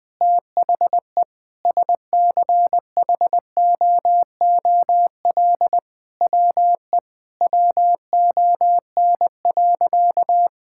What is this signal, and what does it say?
Morse code, heard as a keyed tone, about 21 wpm, 700 Hz. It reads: THE SCHOOL WE WON.
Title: Sentences with 9-Word Phrases - 20wpm